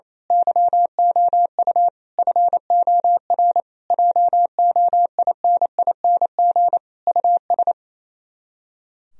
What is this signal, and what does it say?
Morse code, heard as a keyed tone, about 27 wpm, 700 Hz. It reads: YOU FOR JOINING US
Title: Speed-Racing - Sets of 5 Words 28wpm